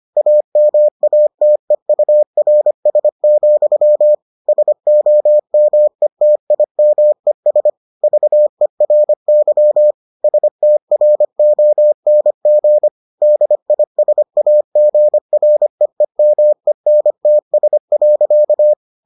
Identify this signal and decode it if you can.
Morse code, heard as a keyed tone, about 25 wpm, 600 Hz. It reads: AMATEURS, SOMETIMES VERY STRONG DISAGREEMENTS.